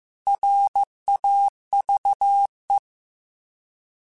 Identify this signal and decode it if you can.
Morse code, heard as a keyed tone, about 15 wpm, 800 Hz. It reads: RAVE